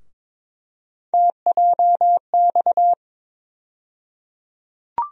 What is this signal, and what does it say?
Morse code, heard as a keyed tone, about 22 wpm, 700 Hz. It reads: TJX
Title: USA Call Sign Postfixes - 3 Letters 22wpm